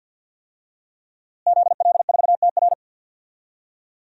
Morse code, heard as a keyed tone, about 48 wpm, 700 Hz. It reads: ZL4TR
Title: DXCC-100 Call Signs - 5 Characters - Repeated 2x - 50wpm